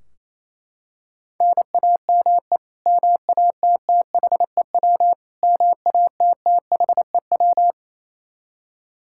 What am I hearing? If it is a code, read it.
Morse code, heard as a keyed tone, about 27 wpm, 700 Hz. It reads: NAME MATTHEW MATTHEW
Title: K1USN SST QSO Practice 28wpm